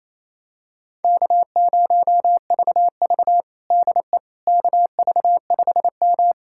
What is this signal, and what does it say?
Morse code, heard as a keyed tone, about 28 wpm, 700 Hz. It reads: K0VV DE KV5M
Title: USA Callsign DE Callsign - 4 Characters - 28wpm